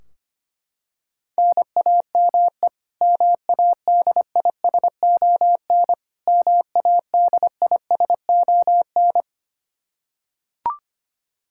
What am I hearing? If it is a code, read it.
Morse code, heard as a keyed tone, about 25 wpm, 700 Hz. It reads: NAME MADISON MADISON
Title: K1USN SST QSO Practice 25wpm